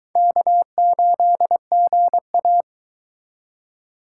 Morse code, heard as a keyed tone, about 22 wpm, 700 Hz. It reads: K8GA